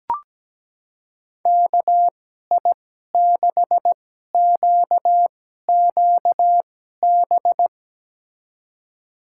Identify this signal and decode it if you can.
Morse code, heard as a keyed tone, about 17 wpm, spaced as for 12 wpm, 700 Hz. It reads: KI6QQB
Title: USA Call Signs - 6 Characters - 2x Character Spacing - 17wpm